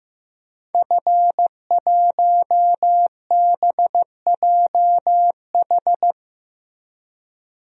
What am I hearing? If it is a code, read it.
Morse code, heard as a keyed tone, about 16 wpm, 700 Hz. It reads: F1BJH
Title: DXCC-100 Call Signs - 5 Characters - Repeated 2x - 15wpm